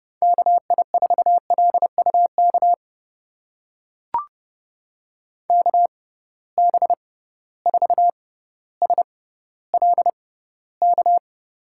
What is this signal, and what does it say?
Morse code, heard as a keyed tone, about 32 wpm, spaced as for 16 wpm, 700 Hz. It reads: KI4LUK KB4SLK
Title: USA Call Signs - 6 Characters - 6x Character Spacing - 30wpm